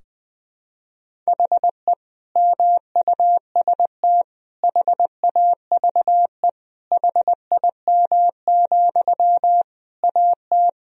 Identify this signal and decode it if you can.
Morse code, heard as a keyed tone, about 21 wpm, 700 Hz. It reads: HE MUST HAVE HIM, AT
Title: Sentences from Top 1000 Words - Review Part 4 - 20wpm